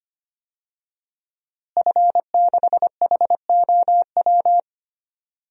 Morse code, heard as a keyed tone, about 25 wpm, 700 Hz. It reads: F6HOW
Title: DXCC-100 Call Signs - 5 Characters - Repeated 2x - 25wpm